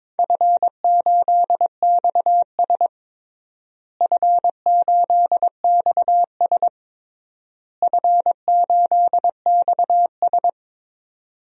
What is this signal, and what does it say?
Morse code, heard as a keyed tone, about 22 wpm, 700 Hz. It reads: F8XS F8XS F8XS
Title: DXCC-100 Call Signs - 4 Characters - Repeated 3x - 22wpm